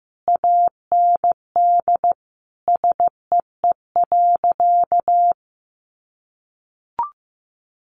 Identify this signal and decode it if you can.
Morse code, heard as a keyed tone, about 15 wpm, 700 Hz. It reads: AND SEE.
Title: Sentences from Top 100 Words 15wpm